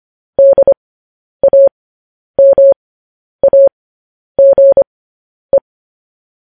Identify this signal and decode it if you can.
Morse code, heard as a keyed tone, about 25 wpm, spaced as for 10 wpm, 550 Hz. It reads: DAMAGE